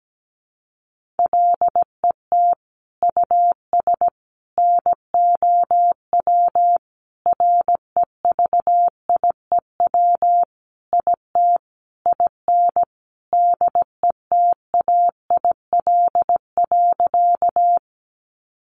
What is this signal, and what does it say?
Morse code, heard as a keyed tone, about 17 wpm, 700 Hz. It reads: LET US NOW REVIEW IT IN DETAIL.
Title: Sentences from Top 1500 Words 17wpm